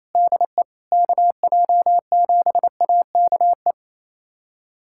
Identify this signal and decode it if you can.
Morse code, heard as a keyed tone, about 27 wpm, 700 Hz. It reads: DE KJ7AKE
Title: USA Callsign DE Callsign - 6 Characters - 28wpm